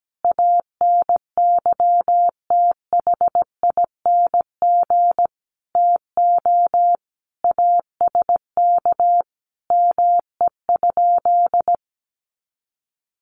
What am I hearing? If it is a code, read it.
Morse code, heard as a keyed tone, about 17 wpm, 700 Hz. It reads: ANYTHING TO ASK ME?
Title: Sentences from Top 1000 Words 17wpm